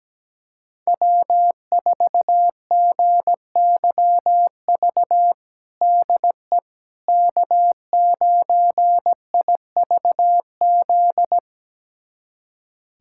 Morse code, heard as a keyed tone, about 17 wpm, 700 Hz. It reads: W4GYV DE K9IVZ
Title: USA Callsign DE Callsign - 5 Characters - 17wpm